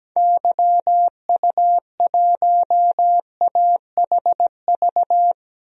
Morse code, heard as a keyed tone, about 17 wpm, 700 Hz. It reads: YU1AHV
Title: DXCC-100 Call Signs - 6 Characters - 17wpm